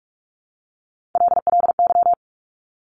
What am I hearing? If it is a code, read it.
Morse code, heard as a keyed tone, about 46 wpm, 700 Hz. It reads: LLY
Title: Top Three Letter Combinations in Words 45wpm